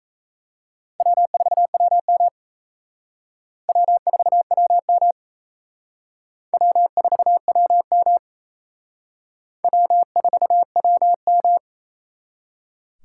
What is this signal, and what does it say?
Morse code, heard as a keyed tone, about 35 wpm, 700 Hz. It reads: W4WM W4WM W4WM W4WM